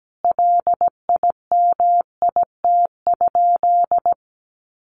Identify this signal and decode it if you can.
Morse code, heard as a keyed tone, about 17 wpm, 700 Hz. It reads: LIMIT?
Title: Sentences from Top 1500 Words 17wpm